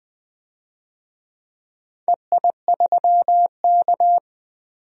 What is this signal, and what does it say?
Morse code, heard as a keyed tone, about 20 wpm, 700 Hz. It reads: EI3K
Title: DXCC-100 Call Signs - 4 Characters - 20wpm